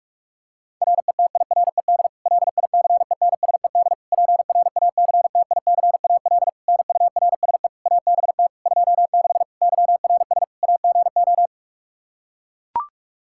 Morse code, heard as a keyed tone, about 42 wpm, 700 Hz. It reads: RETIRED LICENSED PRACTICAL NURSE ABT 26 YRS AGO
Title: Practice QSO - Exchange 3 - More Info - 45wpm